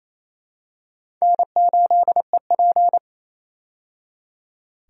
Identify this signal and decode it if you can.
Morse code, heard as a keyed tone, about 28 wpm, 700 Hz. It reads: N8EP
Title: USA Call Signs - 4 Characters - 28wpm